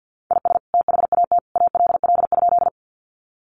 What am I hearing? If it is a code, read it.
Morse code, heard as a keyed tone, about 50 wpm, 700 Hz. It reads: IS THAT ALL?
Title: Sentences from Top 100 Words 50wpm